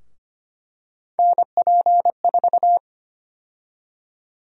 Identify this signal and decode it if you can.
Morse code, heard as a keyed tone, about 25 wpm, 700 Hz. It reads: NP4
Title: USA Call Sign Prefixes - 25wpm